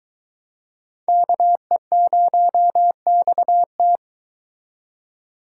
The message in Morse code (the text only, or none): KE0XT